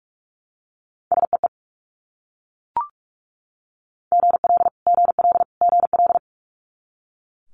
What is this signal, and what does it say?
Morse code, heard as a keyed tone, about 42 wpm, 700 Hz. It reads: SEE GL GL GL
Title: Multi-letter Phonemes Repeated 3x 45wpm